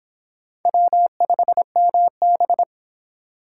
Morse code, heard as a keyed tone, about 24 wpm, 700 Hz. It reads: W5MB